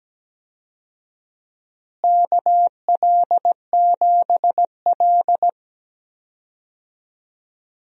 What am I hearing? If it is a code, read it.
Morse code, heard as a keyed tone, about 17 wpm, 700 Hz. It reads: KL7L
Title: DXCC-100 Call Signs - 4 Characters - Repeated 3x - 17wpm